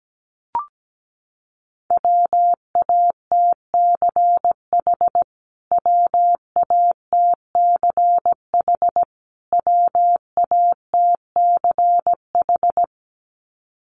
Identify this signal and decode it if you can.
Morse code, heard as a keyed tone, about 17 wpm, 700 Hz. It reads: WATCH WATCH WATCH
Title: Top 500 Words Repeated 17wpm